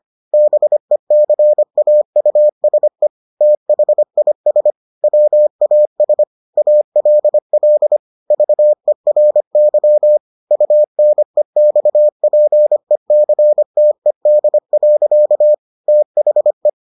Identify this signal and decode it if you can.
Morse code, heard as a keyed tone, about 25 wpm, 600 Hz. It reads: BECAUSE THIS WAS ALL VERY UNEXPECTED. THE